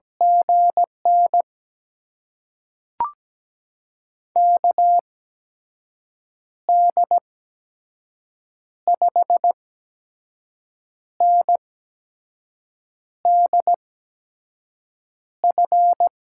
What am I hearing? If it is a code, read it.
Morse code, heard as a keyed tone, about 17 wpm, spaced as for 5 wpm, 700 Hz. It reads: GN KD5NDF
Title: USA Call Signs - 6 Characters - 8x Character Spacing - 17wpm